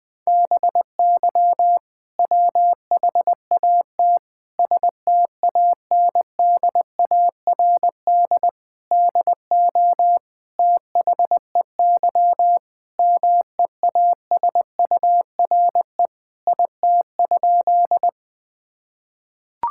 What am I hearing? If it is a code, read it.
Morse code, heard as a keyed tone, about 20 wpm, 700 Hz. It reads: BY WHAT STANDARD DO THEY MEASURE IT?
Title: Sentences from Top 1000 Words - Review Part 6 - 20wpm